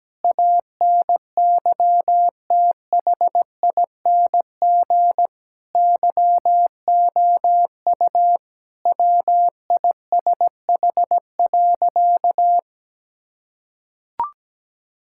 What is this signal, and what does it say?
Morse code, heard as a keyed tone, about 17 wpm, 700 Hz. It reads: ANYTHING YOU WISH.
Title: Sentences from Top 1600 Words 17wpm